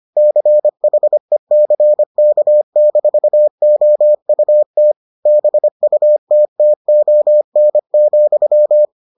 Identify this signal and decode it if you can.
Morse code, heard as a keyed tone, about 25 wpm, 600 Hz. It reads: CHECK-OUT BUTTON,